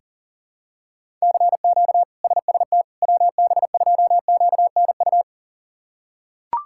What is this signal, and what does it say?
Morse code, heard as a keyed tone, about 40 wpm, 700 Hz. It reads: CQ SST WB2QNU
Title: K1USN SST QSO Practice - 40wpm